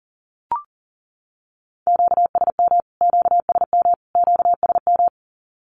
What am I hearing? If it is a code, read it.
Morse code, heard as a keyed tone, about 40 wpm, 700 Hz. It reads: QSM QSM QSM
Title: Q-Codes Repeated 3x 40wpm